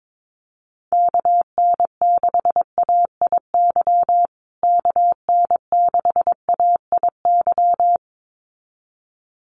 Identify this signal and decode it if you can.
Morse code, heard as a keyed tone, about 22 wpm, 700 Hz. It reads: KN6AIY KN6AIY
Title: USA Call Signs Repeated 2x 22wpm